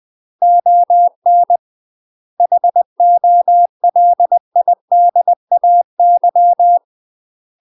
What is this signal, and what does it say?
Morse code, heard as a keyed tone, about 20 wpm, 700 Hz. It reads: ON HOLIDAY
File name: UK MASTER QSOs TRAINING / 20WPM / UK MASTER QSOs 0025